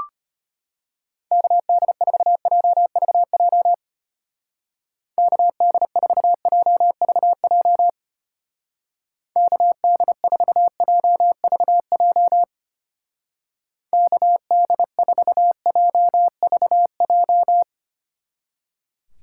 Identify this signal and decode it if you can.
Morse code, heard as a keyed tone, about 31 wpm, 700 Hz. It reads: KD4JVJ KD4JVJ KD4JVJ KD4JVJ
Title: Speed Racing - USA 2x3 Call Signs 25wpm